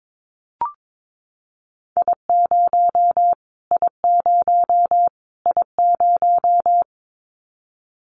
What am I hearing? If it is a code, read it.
Morse code, heard as a keyed tone, about 22 wpm, 700 Hz. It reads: I0 I0 I0
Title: Callsign Prefixes for Top 25 DXCC - Repeated 3x - 22wpm